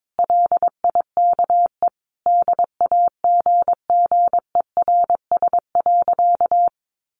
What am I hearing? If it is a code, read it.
Morse code, heard as a keyed tone, about 22 wpm, 700 Hz. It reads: LIKE DAGGERS.
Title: A Sound Of Thunder 22wpm